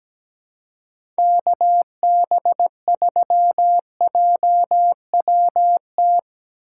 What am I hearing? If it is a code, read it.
Morse code, heard as a keyed tone, about 17 wpm, 700 Hz. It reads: KB3JWT
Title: USA Call Signs - 6 Characters - 4x Character Spacing - 17wpm